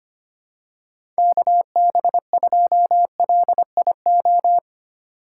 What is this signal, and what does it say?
Morse code, heard as a keyed tone, about 25 wpm, 700 Hz. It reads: KB2LIO
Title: USA Call Signs - 3x Character Spacing - 25wpm